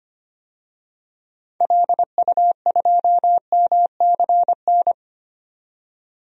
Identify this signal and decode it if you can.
Morse code, heard as a keyed tone, about 25 wpm, 700 Hz. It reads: LU2MCN
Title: DXCC-100 Call Signs - 6 Characters - 25wpm